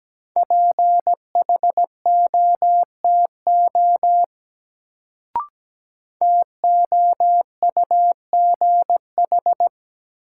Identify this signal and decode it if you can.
Morse code, heard as a keyed tone, about 17 wpm, 700 Hz. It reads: PHOTO TOUGH
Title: Sentences from Top 1200 Words - New Word Drill - 17wpm